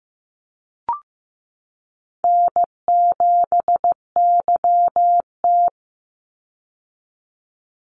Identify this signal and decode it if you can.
Morse code, heard as a keyed tone, about 15 wpm, 700 Hz. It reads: N7YT
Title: Call Signs Repeated 3x - 15wpm